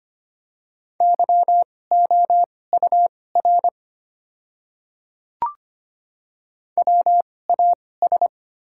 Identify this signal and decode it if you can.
Morse code, heard as a keyed tone, about 25 wpm, spaced as for 18 wpm, 700 Hz. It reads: YOUR WAS
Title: 2x Character Spacing - Top 100 Words - 25wpm